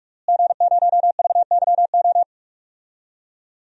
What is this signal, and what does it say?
Morse code, heard as a keyed tone, about 41 wpm, 700 Hz. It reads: G0VYO
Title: DXCC-100 Call Signs - 5 Characters - Repeated 3x - 45wpm